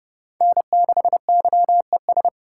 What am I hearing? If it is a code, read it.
Morse code, heard as a keyed tone, about 30 wpm, 700 Hz. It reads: N6YES
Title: USA Call Signs - 3x Character Spacing - 30wpm